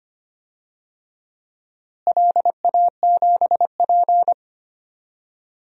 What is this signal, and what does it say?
Morse code, heard as a keyed tone, about 25 wpm, 700 Hz. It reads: LA7P
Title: DXCC-100 Call Signs - 4 Characters - Repeated 2x - 25wpm